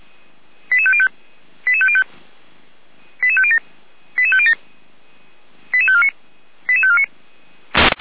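Five-tone selective call (ZVEI), heard as quick runs of five tones, tones of about 70 ms. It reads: 88505 88505 88507 88507 88549 88549
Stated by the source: ZVEI five-tone call received by SDR.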